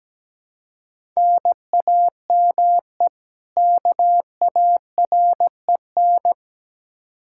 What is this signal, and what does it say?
Morse code, heard as a keyed tone, about 17 wpm, 700 Hz. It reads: NAME KAREN
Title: Practice QSO - Exchange 1 - Beginning - 17wpm